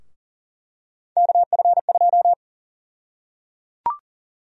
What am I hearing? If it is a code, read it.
Morse code, heard as a keyed tone, about 37 wpm, 700 Hz. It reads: KF2
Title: USA Call Sign Prefixes - 40wpm